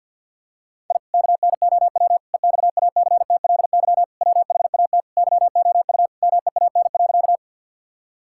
Morse code, heard as a keyed tone, about 50 wpm, 700 Hz. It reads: I KNOW EXACTLY WHAT YOU MEAN.
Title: Sentences from Top 1000 Words - Review Part 6 - 50wpm